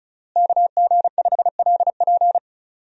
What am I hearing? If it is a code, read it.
Morse code, heard as a keyed tone, about 36 wpm, 700 Hz. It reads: KG5LP
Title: USA Call Signs - 5 Characters - Repeated 3x - 35wpm